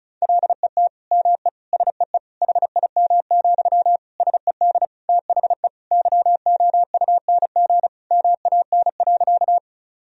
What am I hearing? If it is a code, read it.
Morse code, heard as a keyed tone, about 36 wpm, 700 Hz. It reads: LET ME SEE HIM, SED THE YOUNG MAN.
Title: Sentences from Top 400 Words and Abbreviations 35wpm